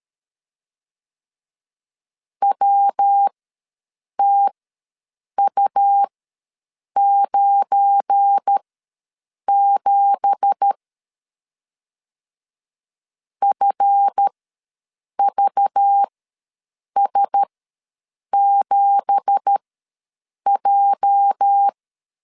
Morse code, heard as a keyed tone, about 13 wpm, spaced as for 7 wpm, 800 Hz. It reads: WTU97 FVS7J
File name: LEZIONI / 9 2867 ws 12wpm 3431